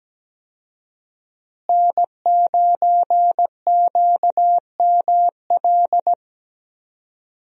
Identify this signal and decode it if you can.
Morse code, heard as a keyed tone, about 17 wpm, 700 Hz. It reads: N9QML
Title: USA Call Signs - 5 Characters - Repeated 2x - 17wpm